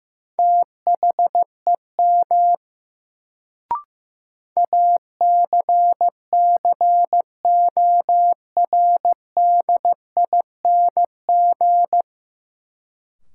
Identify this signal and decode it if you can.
Morse code, heard as a keyed tone, about 15 wpm, 700 Hz. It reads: THEM ACCORDING
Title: Sets of 2 Words - New Word Drill - 15wpm